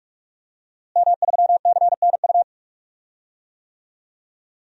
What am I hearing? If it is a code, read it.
Morse code, heard as a keyed tone, about 46 wpm, 700 Hz. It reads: M3CNU
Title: DXCC-100 Call Signs - 5 Characters - Repeated 3x - 45wpm